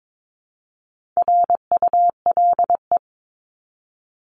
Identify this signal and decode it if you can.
Morse code, heard as a keyed tone, about 22 wpm, 700 Hz. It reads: RULE
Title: ES:S Suffix - Top 100 Words with Base Words 22wpm